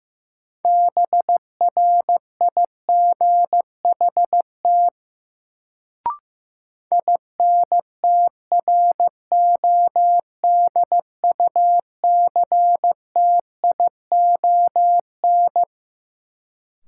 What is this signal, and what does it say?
Morse code, heard as a keyed tone, about 16 wpm, 700 Hz. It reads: BRIGHT INTRODUCTION
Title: Sentences from Top 1000 Words - New Word Drill - 15wpm